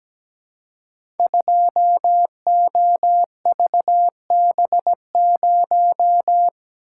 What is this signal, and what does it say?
Morse code, heard as a keyed tone, about 17 wpm, 700 Hz. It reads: 2OVB0
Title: Code Groups - 5 Letter:Numbers - 17wpm